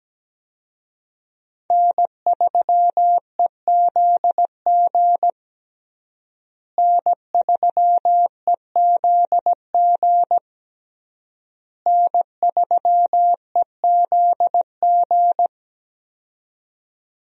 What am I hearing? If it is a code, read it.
Morse code, heard as a keyed tone, about 17 wpm, 700 Hz. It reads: N3EZG N3EZG N3EZG
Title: USA Call Signs - 5 Characters - Repeated 3x - 17wpm